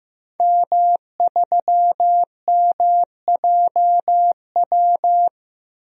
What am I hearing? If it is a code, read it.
Morse code, heard as a keyed tone, about 15 wpm, 700 Hz. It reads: M3MJW
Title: DXCC-100 Call Signs - 5 Characters - 15wpm